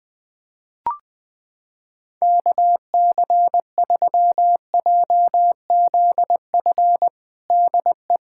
Morse code, heard as a keyed tone, about 20 wpm, 700 Hz. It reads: KC3JZF DE
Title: USA Callsign DE Callsign 20wpm